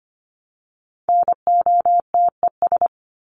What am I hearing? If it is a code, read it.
Morse code, heard as a keyed tone, about 25 wpm, 700 Hz. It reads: NOTES
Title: ES:S Suffix - Top 100 Words with Base Words 25wpm